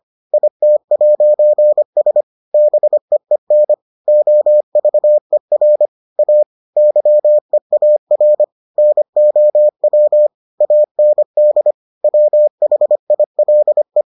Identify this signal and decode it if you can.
Morse code, heard as a keyed tone, about 25 wpm, 600 Hz. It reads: IT'S BEEN OVER A YEAR NOW AND WHILE